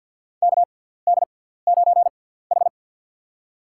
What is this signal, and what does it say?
Morse code, heard as a keyed tone, about 53 wpm, spaced as for 17 wpm, 700 Hz. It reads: KD9H
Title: USA Call Signs - 4 Characters - 6x Character Spacing - 50wpm